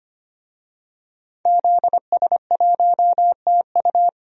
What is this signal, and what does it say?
Morse code, heard as a keyed tone, about 25 wpm, 700 Hz. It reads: ZS1TU
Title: DXCC-100 Call Signs - 5 Characters - 25wpm